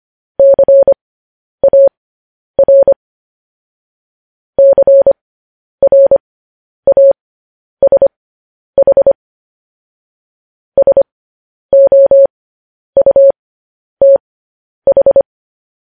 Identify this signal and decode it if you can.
Morse code, heard as a keyed tone, about 25 wpm, spaced as for 10 wpm, 550 Hz. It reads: CAR CRASH SOUTH
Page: news headlines in morse code at 10 wpm